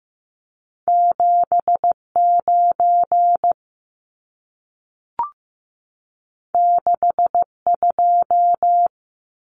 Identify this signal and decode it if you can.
Morse code, heard as a keyed tone, about 15 wpm, 700 Hz. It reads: 79 62
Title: Numbers - 2-Digit 15wpm